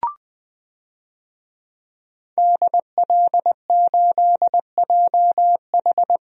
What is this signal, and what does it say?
Morse code, heard as a keyed tone, about 21 wpm, 700 Hz. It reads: DL8JH
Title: DXCC-100 Call Signs - 5 Characters - Repeated 3x - 20wpm